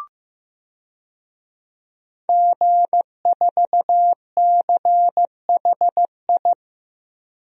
Morse code, heard as a keyed tone, about 15 wpm, 700 Hz. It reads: G4CHI
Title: DXCC-100 Call Signs - 5 Characters - Repeated 2x - 15wpm